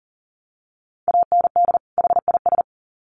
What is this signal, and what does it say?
Morse code, heard as a keyed tone, about 40 wpm, 700 Hz. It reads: AND HIS
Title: Sets of 2 Words 40wpm